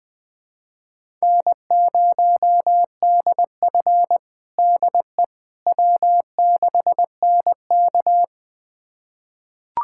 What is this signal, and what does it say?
Morse code, heard as a keyed tone, about 21 wpm, 700 Hz. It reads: N0DF DE W6NK
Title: USA Callsign DE Callsign - 4 Characters - 20wpm